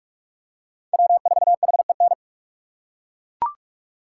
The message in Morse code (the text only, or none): W4HEN